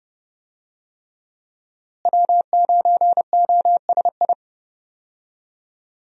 Morse code, heard as a keyed tone, about 30 wpm, 700 Hz. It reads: W9OSI
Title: USA Call Signs - 5 Characters - Repeated 3x - 30wpm